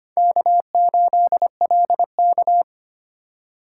Morse code, heard as a keyed tone, about 25 wpm, 700 Hz. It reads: K8LK
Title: USA Call Signs - 2x Character Spacing - 25wpm